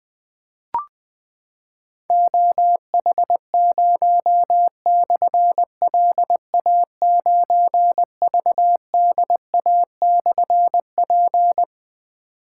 Morse code, heard as a keyed tone, about 21 wpm, 700 Hz. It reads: OH0/LA9VDA/P
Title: Difficult Call Signs 20wpm